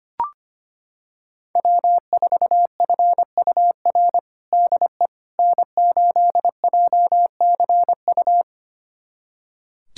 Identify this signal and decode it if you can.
Morse code, heard as a keyed tone, about 25 wpm, 700 Hz. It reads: W4FUR DE N8JCU
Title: USA Callsign DE Callsign - 5 Characters - 25wpm